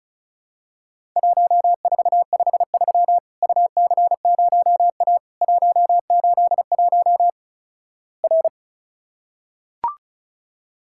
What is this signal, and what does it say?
Morse code, heard as a keyed tone, about 36 wpm, 700 Hz. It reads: 1453 UC0A 181 R
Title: Worked All Europe DX Contest - QTC Exchange - 35wpm